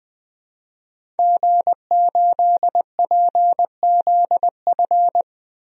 Morse code, heard as a keyed tone, about 20 wpm, 700 Hz. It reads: G8PZF
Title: DXCC-100 Call Signs - 5 Characters - 20wpm